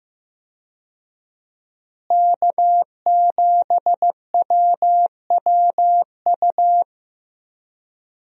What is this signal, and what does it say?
Morse code, heard as a keyed tone, about 15 wpm, 700 Hz. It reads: K7WWU
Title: USA Call Signs - 5 Characters - Repeated 2x - 15wpm